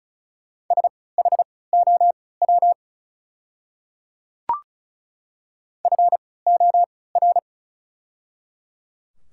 Morse code, heard as a keyed tone, about 36 wpm, spaced as for 20 wpm, 700 Hz. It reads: SHOW FOR
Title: 3x Character Spacing - Top 200 Words - 35wpm